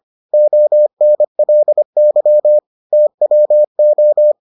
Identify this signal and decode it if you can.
Morse code, heard as a keyed tone, about 25 wpm, 600 Hz. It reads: ONLY TWO